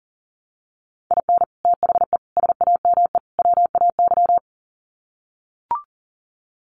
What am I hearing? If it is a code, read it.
Morse code, heard as a keyed tone, about 40 wpm, 700 Hz. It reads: IN THE SAME WAY
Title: Sets of 4 Words 40wpm